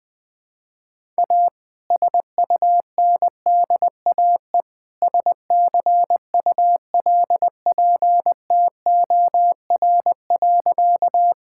Morse code, heard as a keyed tone, about 20 wpm, 700 Hz. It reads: A SUNDAE SCULPTOR.
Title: All-Age Appropriate Jokes - 20wpm